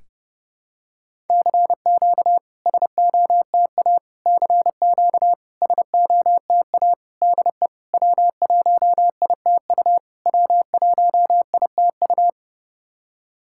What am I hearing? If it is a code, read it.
Morse code, heard as a keyed tone, about 30 wpm, 700 Hz. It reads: CQ SOTA CQ SOTA DE W1ITU W1ITU
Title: SOTA QSO Practice 30wpm